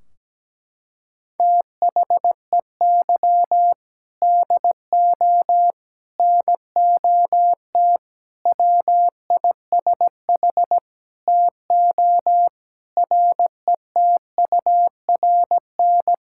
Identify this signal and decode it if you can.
Morse code, heard as a keyed tone, about 17 wpm, 700 Hz. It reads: THEY DO NOT WISH TO RETURN
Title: Sentences from Top 1600 Words 17wpm